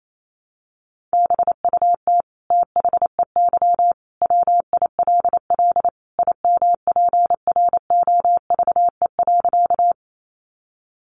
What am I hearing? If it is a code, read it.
Morse code, heard as a keyed tone, about 28 wpm, 700 Hz. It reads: BUT THEY WILL IMPROVE.
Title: Sentences from Top 800 Words 28wpm